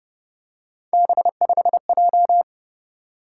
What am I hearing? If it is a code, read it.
Morse code, heard as a keyed tone, about 28 wpm, 700 Hz. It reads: B5J